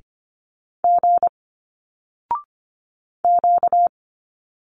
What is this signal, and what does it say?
Morse code, heard as a keyed tone, about 25 wpm, 700 Hz. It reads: G Q
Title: Single Letters 25wpm